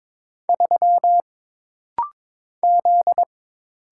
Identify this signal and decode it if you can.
Morse code, heard as a keyed tone, about 22 wpm, 700 Hz. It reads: 3 Z
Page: This rapid-fire timing is geared to push your ICR (Instant Character Recognition) skills.